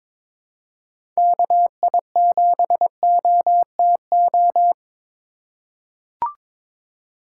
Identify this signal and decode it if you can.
Morse code, heard as a keyed tone, about 22 wpm, 700 Hz. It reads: KI7OTO
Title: USA Call Signs - 6 Characters - 8x Character Spacing - 22wpm